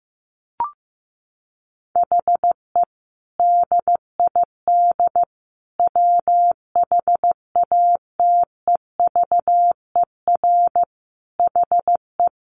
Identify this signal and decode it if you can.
Morse code, heard as a keyed tone, about 15 wpm, 700 Hz. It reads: HE DID WHATEVER HE
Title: Sentences from Top 1100 Words 15wpm